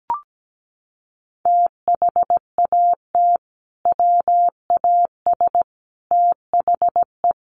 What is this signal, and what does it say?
Morse code, heard as a keyed tone, about 17 wpm, 700 Hz. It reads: THAT WAS THE
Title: Sentences from Top 300 Words 17wpm